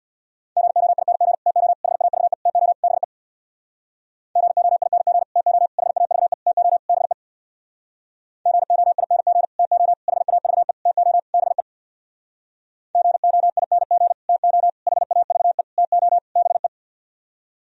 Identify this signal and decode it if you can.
Morse code, heard as a keyed tone, about 64 wpm, 700 Hz. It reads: GOING TO HAVE TO BE GOING TO HAVE TO BE GOING TO HAVE TO BE GOING TO HAVE TO BE